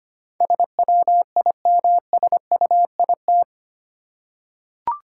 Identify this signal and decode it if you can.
Morse code, heard as a keyed tone, about 25 wpm, 690 Hz. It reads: SWIMSUIT
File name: practice / 25wpm / Easy Words with Hints - 25wpm